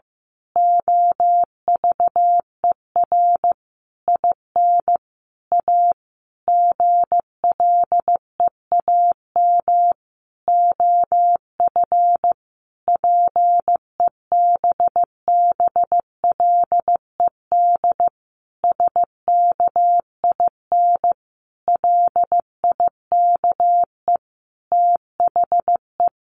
Morse code, heard as a keyed tone, about 15 wpm, 700 Hz. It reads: OVER IN A GLEAM OF PEBBLED SKIN LIKE THE
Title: A Sound Of Thunder 15wpm